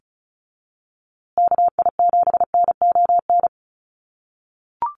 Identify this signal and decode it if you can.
Morse code, heard as a keyed tone, about 34 wpm, 700 Hz. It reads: KI7NON
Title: USA Call Signs Repeated 3x 35wpm